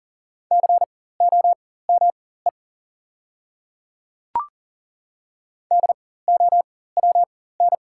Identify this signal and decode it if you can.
Morse code, heard as a keyed tone, about 40 wpm, spaced as for 19 wpm, 690 Hz. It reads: COME DOWN
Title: 4x Character Spacing - Top 100 Words - 40wpm